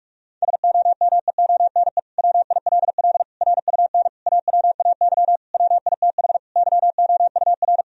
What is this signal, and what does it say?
Morse code, heard as a keyed tone, about 44 wpm, 700 Hz. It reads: SOMEONE WILL RUN AWAY WITH YOUR
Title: Sentences from Top 1000 Words - Review Part 1 - 45wpm